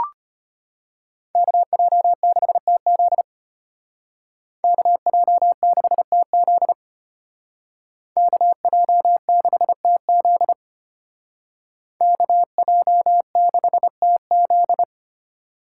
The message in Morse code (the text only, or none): KJ6TZ KJ6TZ KJ6TZ KJ6TZ